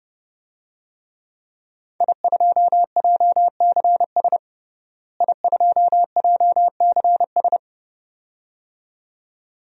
Morse code, heard as a keyed tone, about 30 wpm, 700 Hz. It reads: I2JCS I2JCS
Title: DXCC-100 Call Signs - 5 Characters - Repeated 2x - 30wpm